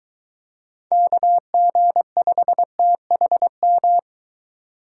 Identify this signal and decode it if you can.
Morse code, heard as a keyed tone, about 22 wpm, 700 Hz. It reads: KG5THM